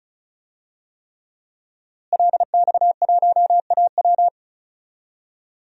Morse code, heard as a keyed tone, about 36 wpm, 700 Hz. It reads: LX1AW
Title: DXCC-100 Call Signs - 5 Characters - Repeated 3x - 35wpm